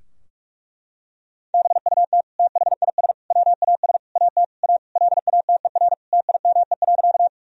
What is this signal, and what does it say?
Morse code, heard as a keyed tone, about 46 wpm, 700 Hz. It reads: BUT THIS WAS AT A LATER TIME.
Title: Sentences from Top 1000 Words - Review Part 4 - 45wpm